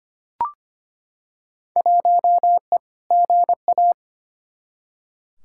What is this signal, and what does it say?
Morse code, heard as a keyed tone, about 25 wpm, 700 Hz. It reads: 1E GA
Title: Field Day Exchanges - 25wpm